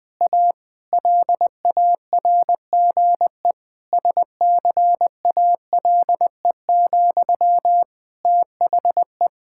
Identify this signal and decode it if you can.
Morse code, heard as a keyed tone, about 21 wpm, 700 Hz. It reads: A LARGE SCALE, THE
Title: Sentences from Top 1000 Words - Review Part 5 - 20wpm